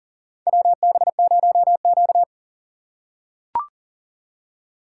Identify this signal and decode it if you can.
Morse code, heard as a keyed tone, about 40 wpm, 700 Hz. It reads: WB0Q
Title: USA Call Signs - 4 Characters - 2x Character Spacing - 40wpm